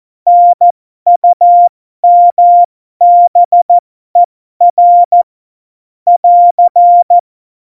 Morse code, heard as a keyed tone, about 14 wpm, spaced as for 12 wpm, 700 Hz. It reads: NUMBER <AR>
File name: Morsekurs / Witze